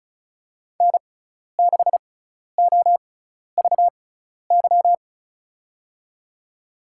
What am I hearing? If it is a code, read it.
Morse code, heard as a keyed tone, about 36 wpm, spaced as for 12 wpm, 700 Hz. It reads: N6OVY
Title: USA Call Signs - 5 Characters - 6x Character Spacing - 35wpm